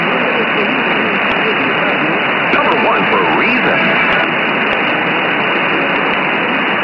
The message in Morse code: none

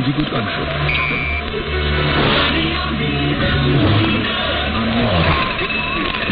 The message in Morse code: none

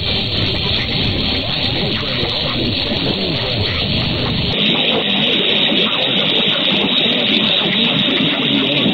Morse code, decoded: none

